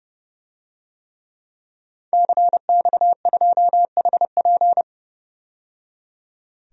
Morse code, heard as a keyed tone, about 30 wpm, 700 Hz. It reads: CX2HP
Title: DXCC-100 Call Signs - 5 Characters - 30wpm